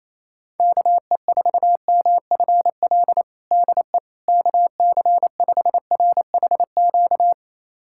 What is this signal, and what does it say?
Morse code, heard as a keyed tone, about 27 wpm, 700 Hz. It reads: KE4MFL DE KC5RHQ
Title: USA Callsign DE Callsign 28wpm